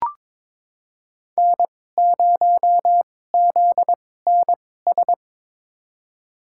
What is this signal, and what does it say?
Morse code, heard as a keyed tone, about 22 wpm, spaced as for 16 wpm, 700 Hz. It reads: N0ZNS
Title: USA Call Signs - 5 Characters - 2x Character Spacing - 22wpm